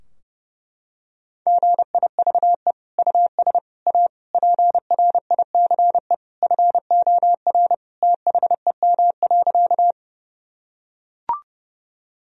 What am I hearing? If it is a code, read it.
Morse code, heard as a keyed tone, about 30 wpm, 700 Hz. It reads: GIVE US A PRICE FOR THEM.
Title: Sentences from Top 1000 Words - Review Part 4 - 30wpm